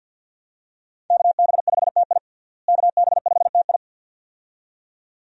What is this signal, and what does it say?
Morse code, heard as a keyed tone, about 52 wpm, 700 Hz. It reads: KB5TI KB5TI
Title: USA Call Signs - 5 Characters - Repeated 2x - 50wpm